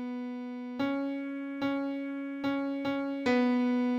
There is no code.